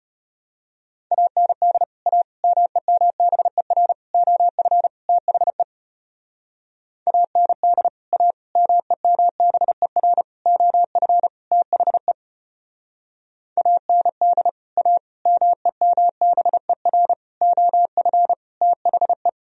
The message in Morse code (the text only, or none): AND A MEMBER OF THE AND A MEMBER OF THE AND A MEMBER OF THE